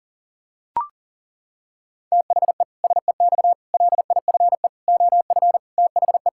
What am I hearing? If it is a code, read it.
Morse code, heard as a keyed tone, about 40 wpm, 700 Hz. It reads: THE SEX LIFE OF THE
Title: Pangram Challenge v2 - 40wpm